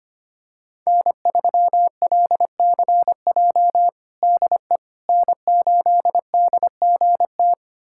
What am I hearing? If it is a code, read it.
Morse code, heard as a keyed tone, about 25 wpm, 700 Hz. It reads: N3LCJ DE N8DGT
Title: USA Callsign DE Callsign 25wpm